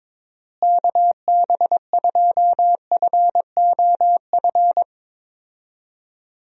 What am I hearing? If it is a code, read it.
Morse code, heard as a keyed tone, about 22 wpm, 700 Hz. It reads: KB2FOF
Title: USA Call Signs - 6 Characters - Repeated 2x - 22wpm